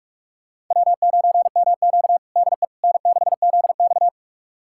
Morse code, heard as a keyed tone, about 42 wpm, 700 Hz. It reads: W9MQ DE N6ZX
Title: USA Callsign DE Callsign - 4 Characters - 45wpm